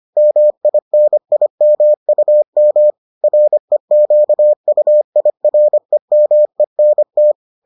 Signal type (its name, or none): Morse code